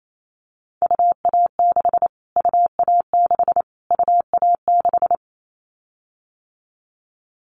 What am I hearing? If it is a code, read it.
Morse code, heard as a keyed tone, about 28 wpm, 700 Hz. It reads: UA6 UA6 UA6
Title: Callsign Prefixes for Top 25 DXCC - Repeated 3x - 28wpm